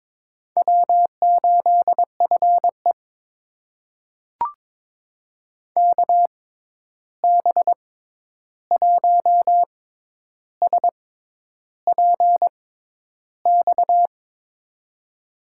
Morse code, heard as a keyed tone, about 22 wpm, spaced as for 10 wpm, 700 Hz. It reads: W8FE KB1SPX
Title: USA Call Signs - 6x Character Spacing - 22wpm